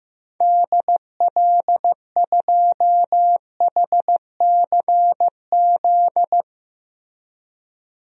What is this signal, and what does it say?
Morse code, heard as a keyed tone, about 15 wpm, 700 Hz. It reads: DL2HCZ
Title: DXCC-100 Call Signs - 6 Characters - 15wpm